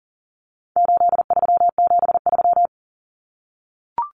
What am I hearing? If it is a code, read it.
Morse code, heard as a keyed tone, about 40 wpm, 700 Hz. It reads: 8373
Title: Numbers - 4-Digit 40wpm